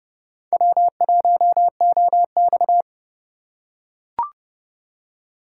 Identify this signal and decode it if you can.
Morse code, heard as a keyed tone, about 29 wpm, 700 Hz. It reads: W1OX